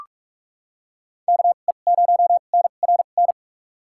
Morse code, heard as a keyed tone, about 45 wpm, spaced as for 32 wpm, 700 Hz. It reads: KE0NRN
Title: USA Call Signs - 6 Characters - 2x Character Spacing - 45wpm